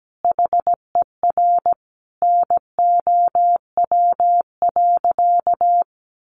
Morse code, heard as a keyed tone, about 17 wpm, 700 Hz. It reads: HER NOW.
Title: Sentences from Top 1400 Words 17wpm